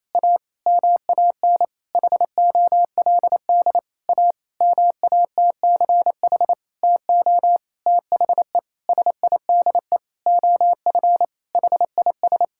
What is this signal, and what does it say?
Morse code, heard as a keyed tone, about 28 wpm, 700 Hz. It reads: A MAN HOLD A MATCH TO THE SIDE OF HIS
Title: Sentences from Top 1600 Words 28wpm